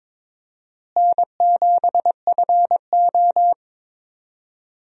Morse code, heard as a keyed tone, about 22 wpm, 700 Hz. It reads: N7FO
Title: USA Call Signs - 4 Characters - Repeated 2x - 22wpm